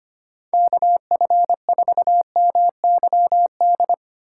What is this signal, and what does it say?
Morse code, heard as a keyed tone, about 30 wpm, 700 Hz. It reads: KF4MYD
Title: Speed Racing - USA 2x3 Call Signs 25wpm